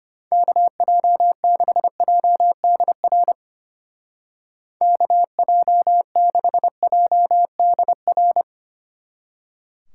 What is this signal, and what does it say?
Morse code, heard as a keyed tone, about 31 wpm, 700 Hz. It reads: KJ6JDR KJ6JDR
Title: Speed Racing - USA 2x3 Call Signs 25wpm